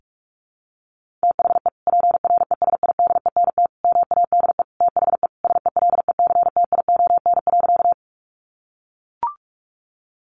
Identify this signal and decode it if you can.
Morse code, heard as a keyed tone, about 44 wpm, 700 Hz. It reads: THE PRESIDENT MADE THE SELECTION.
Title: Sentences from Top 1400 Words 45wpm